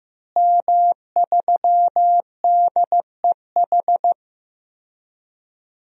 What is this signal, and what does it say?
Morse code, heard as a keyed tone, about 15 wpm, 700 Hz. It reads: M3DEH
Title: DXCC-100 Call Signs - 5 Characters - Repeated 2x - 15wpm